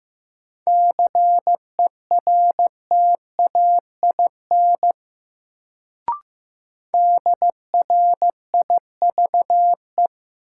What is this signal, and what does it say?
Morse code, heard as a keyed tone, about 15 wpm, 700 Hz. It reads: CERTAIN DRIVE
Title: Sentences from Top 500 Words - New Word Drill - 15wpm